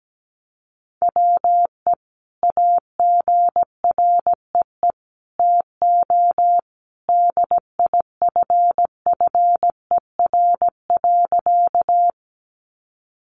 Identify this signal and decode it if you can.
Morse code, heard as a keyed tone, about 17 wpm, 700 Hz. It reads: WE AGREE TO DIFFER.
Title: Sentences from Top 1100 Words 17wpm